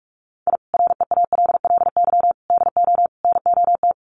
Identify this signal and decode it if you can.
Morse code, heard as a keyed tone, about 44 wpm, 700 Hz. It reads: I REALLY DO NOT
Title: Sets of 4 Words 45wpm